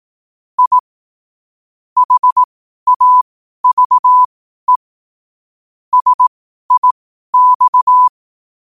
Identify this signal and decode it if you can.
Morse code, heard as a keyed tone, about 18 wpm, spaced as for 12 wpm, 1000 Hz. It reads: I HAVE SIX
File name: 12wpm